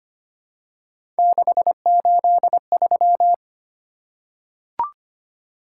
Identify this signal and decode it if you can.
Morse code, heard as a keyed tone, about 25 wpm, 700 Hz. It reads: 683
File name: Three Letter-Number Combinations - 25wpm